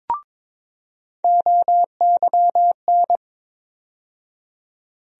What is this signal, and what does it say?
Morse code, heard as a keyed tone, about 22 wpm, 700 Hz. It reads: OYN
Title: USA Call Sign Postfixes - 3 Letters 22wpm